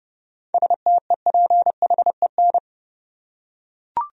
Morse code, heard as a keyed tone, about 30 wpm, 700 Hz. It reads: STEPHEN
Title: USA Common Names - 30wpm